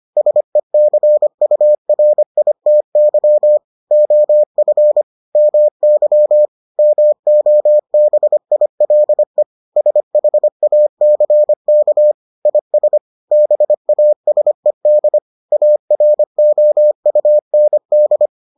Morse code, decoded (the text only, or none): SECURITY OF MY MOBILE SHACK IS BASED AROUND